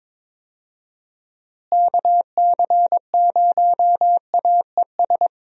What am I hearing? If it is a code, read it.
Morse code, heard as a keyed tone, about 22 wpm, 700 Hz. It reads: KC0AES
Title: USA Call Signs - 6 Characters - Repeated 3x - 22wpm